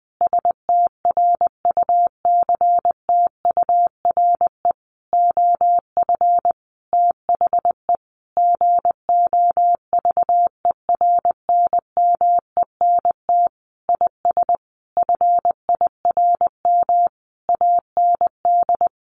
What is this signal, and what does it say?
Morse code, heard as a keyed tone, about 20 wpm, 700 Hz. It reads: STRUCTURE OF THE GOVERNMENT IS FIRM AND
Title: Sentences from Top 1500 Words 20wpm